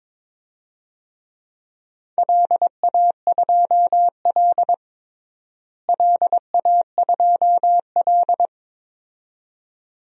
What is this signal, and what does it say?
Morse code, heard as a keyed tone, about 22 wpm, 700 Hz. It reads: LA2L LA2L
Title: DXCC-100 Call Signs - 4 Characters - Repeated 2x - 22wpm